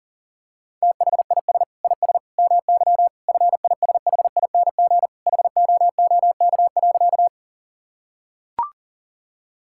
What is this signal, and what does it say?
Morse code, heard as a keyed tone, about 40 wpm, 700 Hz. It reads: THIS IS MY FISHING HOOK.
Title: Sentences from Top 1600 Words 40wpm